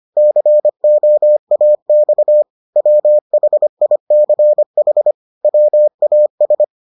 Morse code, heard as a keyed tone, about 25 wpm, 600 Hz. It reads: COAX WHICH WAS